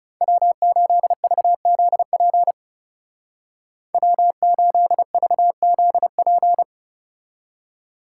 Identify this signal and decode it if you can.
Morse code, heard as a keyed tone, about 40 wpm, 700 Hz. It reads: W8VZP W8VZP